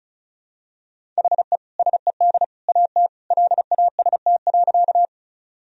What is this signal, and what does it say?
Morse code, heard as a keyed tone, about 36 wpm, 700 Hz. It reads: HE SED AT LAST.
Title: Sentences from Top 300 Words and Abbreviations 35wpm